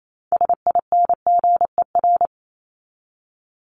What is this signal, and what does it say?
Morse code, heard as a keyed tone, about 28 wpm, 700 Hz. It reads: SINGER
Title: Top 1501-1750 Words 28wpm